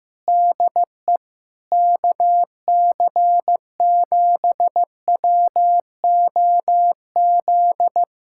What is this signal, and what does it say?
Morse code, heard as a keyed tone, about 15 wpm, 700 Hz. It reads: DE KC7WOZ
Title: Speed-Racing Common QSO Phrases Encore - 15wpm